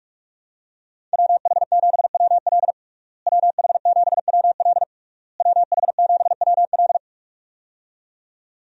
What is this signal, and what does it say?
Morse code, heard as a keyed tone, about 46 wpm, 700 Hz. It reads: WH7WL WH7WL WH7WL
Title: USA Call Signs - 5 Characters - Repeated 3x - 45wpm